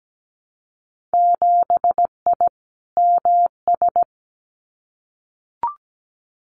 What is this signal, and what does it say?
Morse code, heard as a keyed tone, about 17 wpm, 700 Hz. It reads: 7I MS
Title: Winter Field Day Exchanges Encore 17wpm